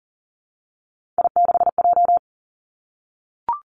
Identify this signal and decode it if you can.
Morse code, heard as a keyed tone, about 41 wpm, 700 Hz. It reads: I6J
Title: Three Challenging Letter-Number Combinations 40wpm